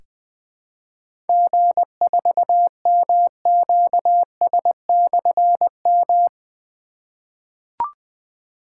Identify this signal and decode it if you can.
Morse code, heard as a keyed tone, about 20 wpm, 700 Hz. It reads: G4MQS/M
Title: Difficult Call Signs 20wpm